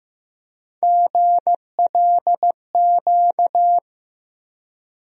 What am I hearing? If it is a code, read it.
Morse code, heard as a keyed tone, about 15 wpm, 700 Hz. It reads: GLQ
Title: USA Call Sign Postfixes - 3 Letters 15wpm